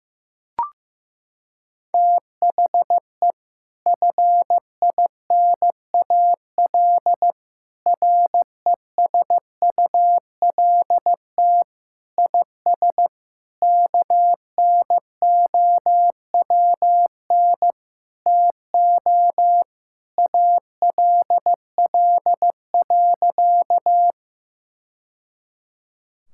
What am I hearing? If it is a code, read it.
Morse code, heard as a keyed tone, about 15 wpm, 700 Hz. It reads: THE FINAL RESULT IS KNOWN TO ALL.
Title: Sentences from Top 1000 Words - Review Part 5 - 15wpm